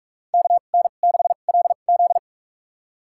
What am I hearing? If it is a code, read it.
Morse code, heard as a keyed tone, about 46 wpm, spaced as for 33 wpm, 700 Hz. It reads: KN6LZ
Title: USA Call Signs - 5 Characters - 2x Character Spacing - 45wpm